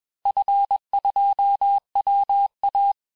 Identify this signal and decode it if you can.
Morse code, heard as a keyed tone, about 21 wpm, 800 Hz. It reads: F2WA
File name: teste-FIN-de-QSO-vitesse-20mpm